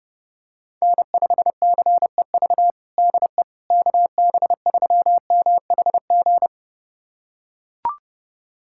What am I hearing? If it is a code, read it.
Morse code, heard as a keyed tone, about 32 wpm, 700 Hz. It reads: N5CEV DE KB3MHG
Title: USA Callsign DE Callsign 30wpm